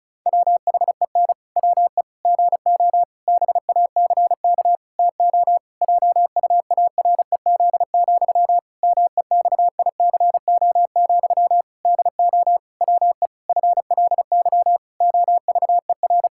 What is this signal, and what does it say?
Morse code, heard as a keyed tone, about 36 wpm, 700 Hz. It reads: WHEN WE GO BACK TO JUAREZ, MEXICO, DO WE FLY OVER
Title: Pangram Challenge v2 - 35wpm